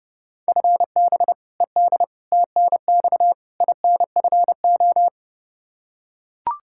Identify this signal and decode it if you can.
Morse code, heard as a keyed tone, about 30 wpm, 700 Hz. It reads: FB ED TNX INFO
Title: K1USN SST QSO Practice 30wpm